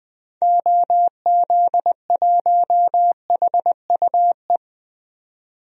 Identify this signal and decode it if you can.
Morse code, heard as a keyed tone, about 20 wpm, 700 Hz. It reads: OZ1HUE
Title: DXCC-100 Call Signs - 6 Characters - 20wpm